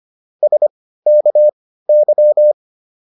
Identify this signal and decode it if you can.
Morse code, heard as a keyed tone, about 25 wpm, spaced as for 15 wpm, 600 Hz. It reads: SKY